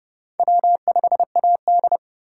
Morse code, heard as a keyed tone, about 36 wpm, 700 Hz. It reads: W5AD
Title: Speed Racing - USA Call Signs 30wpm